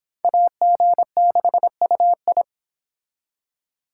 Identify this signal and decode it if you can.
Morse code, heard as a keyed tone, about 26 wpm, 700 Hz. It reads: AG6UI